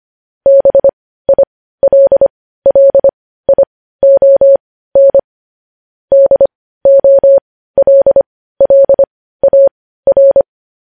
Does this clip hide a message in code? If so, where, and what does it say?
Yes: Morse code from 0.5 to 10.4 s, heard as a keyed tone, about 25 wpm, spaced as for 15 wpm, 550 Hz. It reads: BILLION DOLLAR